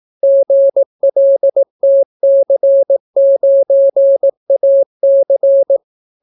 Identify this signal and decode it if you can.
Morse code, heard as a keyed tone, about 18 wpm, 550 Hz. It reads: GLTC9AC